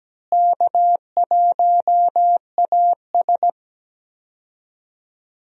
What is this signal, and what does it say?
Morse code, heard as a keyed tone, about 17 wpm, 700 Hz. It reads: K1AS
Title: USA Call Signs - 4 Characters - Repeated 3x - 17wpm